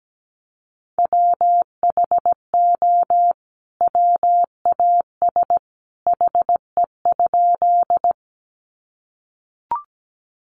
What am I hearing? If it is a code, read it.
Morse code, heard as a keyed tone, about 17 wpm, 700 Hz. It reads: WHO WAS HE?
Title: Sentences from Top 100 Words 17wpm